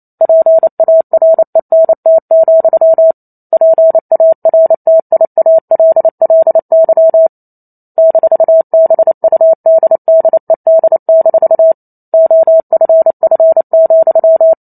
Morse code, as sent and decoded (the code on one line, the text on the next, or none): .--. .- .-. -. - --..-- / .--. .- .-. - .. .- .-.. .-.. -.-- / -....- -... ..- -.. -.. -.. -....- / --- ..-. ..-. --..--
PARENT, PARTIALLY -BUDDED- OFF,